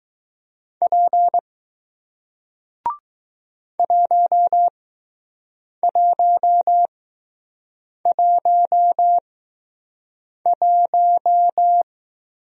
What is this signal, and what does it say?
Morse code, heard as a keyed tone, about 20 wpm, spaced as for 6 wpm, 700 Hz. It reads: P 1111